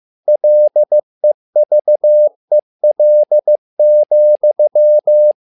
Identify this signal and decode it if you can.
Morse code, heard as a keyed tone, about 15 wpm, 600 Hz. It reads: LEVEL,
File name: learning center / tower trouble / tower trouble 15wpm